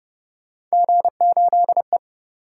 Morse code, heard as a keyed tone, about 30 wpm, 700 Hz. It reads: G8E